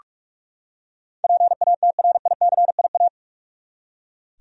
Morse code, heard as a keyed tone, about 46 wpm, 700 Hz. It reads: PATRICIA
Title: USA Common Names - 45wpm